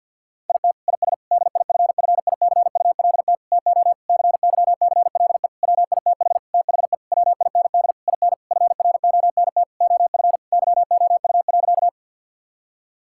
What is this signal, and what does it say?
Morse code, heard as a keyed tone, about 51 wpm, 700 Hz. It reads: IT IS DIFFICULT TO CYCLE WITH THE WIND IN FRONT OF YOU.
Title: Sentences from Top 1000 Words - Review Part 6 - 50wpm